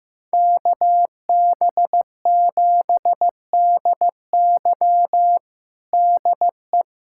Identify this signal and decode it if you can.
Morse code, heard as a keyed tone, about 15 wpm, 700 Hz. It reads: KB7DY DE
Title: USA Callsign DE Callsign - 5 Characters - 15wpm